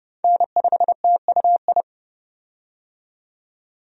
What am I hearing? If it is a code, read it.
Morse code, heard as a keyed tone, about 32 wpm, 700 Hz. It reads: N5TUI
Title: USA Call Signs - 5 Characters - 30wpm